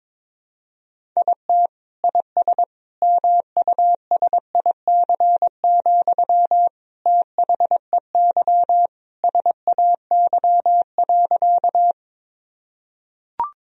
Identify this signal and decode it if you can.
Morse code, heard as a keyed tone, about 22 wpm, 700 Hz. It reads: IT IS MUSIC, THEY SAY.
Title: Sentences from Top 1000 Words - Review Part 1 - 22wpm